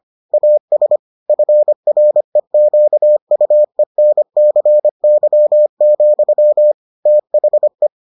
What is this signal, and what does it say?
Morse code, heard as a keyed tone, about 25 wpm, 600 Hz. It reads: AS FREQUENCY, THE